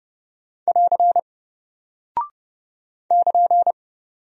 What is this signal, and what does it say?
Morse code, heard as a keyed tone, about 30 wpm, 700 Hz. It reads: <AR> <KN>
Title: Single Character (Letter:Number:Punctuation:Prosign) Alt 30wpm